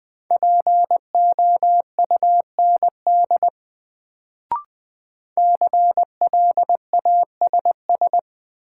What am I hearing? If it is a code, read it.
Morse code, heard as a keyed tone, about 20 wpm, 700 Hz. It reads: POUND CLASS
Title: Sentences from Top 1000 Words - Review Part 2 - New Word Drill - 20wpm